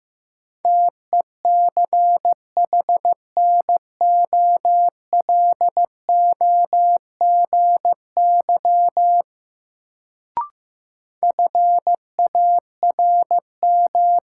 Morse code, heard as a keyed tone, about 15 wpm, 700 Hz. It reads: TECHNOLOGY FARM
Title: Sentences from Top 1200 Words - New Word Drill Alt - 15wpm